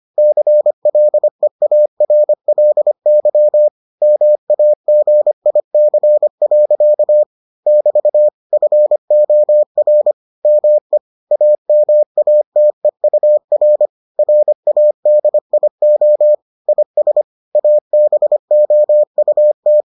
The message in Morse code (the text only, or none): CLEARLY MAGIC. <BT> FOR ME AMATEUR RADIO IS ABOUT